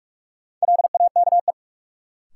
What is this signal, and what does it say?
Morse code, heard as a keyed tone, about 44 wpm, 700 Hz. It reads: LAKE
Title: Sentences from Top 1100 Words - New Word Drill Alt - 45wpm